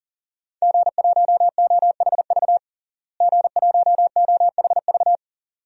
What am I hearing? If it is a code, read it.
Morse code, heard as a keyed tone, about 40 wpm, 700 Hz. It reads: G1OHV G1OHV
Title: DXCC-100 Call Signs - 5 Characters - Repeated 2x - 40wpm